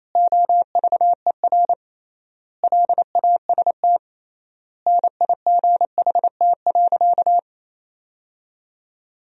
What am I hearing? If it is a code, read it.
Morse code, heard as a keyed tone, about 28 wpm, 700 Hz. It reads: OVER LAST NIGHT.
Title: 3x Word-Spacing - Sentences from Top 300 Words - 28wpm